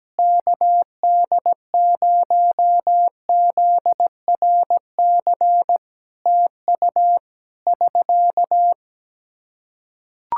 Morse code, heard as a keyed tone, about 17 wpm, 700 Hz. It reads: KD0ZRC TU <SK>
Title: Practice QSO - Exchange 4 - Ending - 17wpm